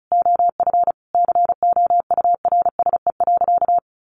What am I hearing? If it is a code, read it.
Morse code, heard as a keyed tone, about 34 wpm, 700 Hz. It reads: OF COURSE.
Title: Sentences from Top 800 Words 35wpm